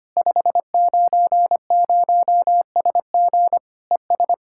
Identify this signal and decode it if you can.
Morse code, heard as a keyed tone, about 25 wpm, 700 Hz. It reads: 590SG ES
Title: Practice QSO - Exchange 2 - Info - 25wpm